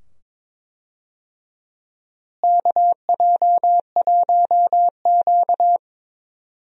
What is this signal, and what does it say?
Morse code, heard as a keyed tone, about 22 wpm, 700 Hz. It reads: KJ1Q
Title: USA Call Signs - 4 Characters - Repeated 2x - 22wpm